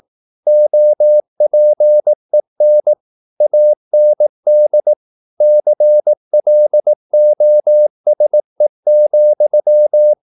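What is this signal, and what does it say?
Morse code, heard as a keyed tone, about 18 wpm, 600 Hz. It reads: OPEN AND CLOSE,
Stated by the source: Morse practice at 18 wpm, English text, from CW med Gnister.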